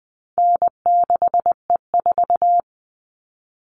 Morse code, heard as a keyed tone, about 20 wpm, 700 Hz. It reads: N6E4
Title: All Cut Numbers - 4-Digit 20wpm